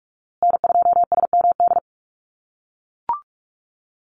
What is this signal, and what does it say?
Morse code, heard as a keyed tone, about 44 wpm, 700 Hz. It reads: N2SMD
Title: USA Call Signs Repeated 2x 45wpm